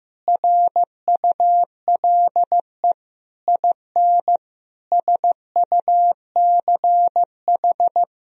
Morse code, heard as a keyed tone, about 15 wpm, 700 Hz. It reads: RULE IN SUCH
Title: Sentences from Top 1000 Words - Review Part 2 - 15wpm